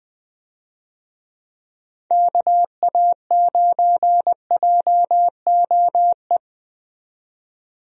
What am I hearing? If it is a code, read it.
Morse code, heard as a keyed tone, about 20 wpm, 700 Hz. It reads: KA9JOE
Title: USA Call Signs - 6 Characters - 20wpm